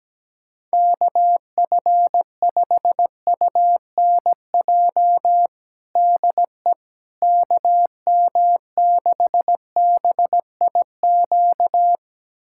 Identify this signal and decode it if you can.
Morse code, heard as a keyed tone, about 17 wpm, 700 Hz. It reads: KF5UNJ DE KM6BIQ
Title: USA Callsign DE Callsign 17wpm